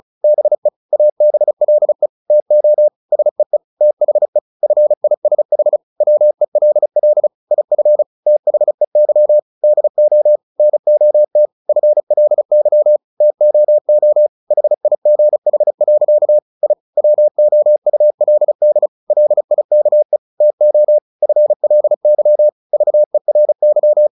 Morse code, heard as a keyed tone, about 34 wpm, 600 Hz. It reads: BE ABLE TO SEE THE FISH WELL IF THEY DO NOT FLY TOO HIGH. I WOULD LIKE TO FLY VERY
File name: the old man and the sea 2 35wpm